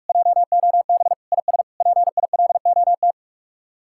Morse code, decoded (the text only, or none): JOB IS PILOT